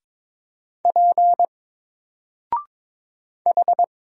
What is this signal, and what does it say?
Morse code, heard as a keyed tone, about 22 wpm, 700 Hz. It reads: P H
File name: Single-Letter-Number-22wpm